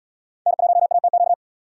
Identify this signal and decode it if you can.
Morse code, heard as a keyed tone, about 70 wpm, 700 Hz. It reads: N1ATQ